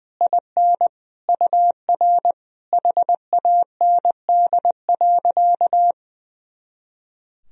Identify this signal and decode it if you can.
Morse code, heard as a keyed tone, about 20 wpm, 700 Hz. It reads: IN UR HAND.
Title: Sentences from Top 400 Words and Abbreviations 20wpm